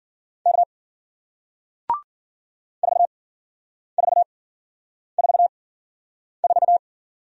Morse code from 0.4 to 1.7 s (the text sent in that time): K